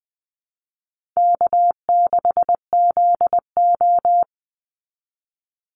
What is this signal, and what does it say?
Morse code, heard as a keyed tone, about 20 wpm, 700 Hz. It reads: K6ZO
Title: Easy Call Signs Repeated 2x - 20wpm